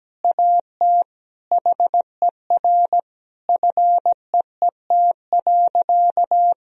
Morse code, heard as a keyed tone, about 17 wpm, 700 Hz. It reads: AT HER FEET.
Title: Sentences from Top 1000 Words - Review Part 1 - 17wpm